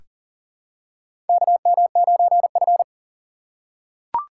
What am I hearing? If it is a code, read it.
Morse code, heard as a keyed tone, about 40 wpm, 700 Hz. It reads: KM9F
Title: USA Call Signs - 4 Characters - 2x Character Spacing - 40wpm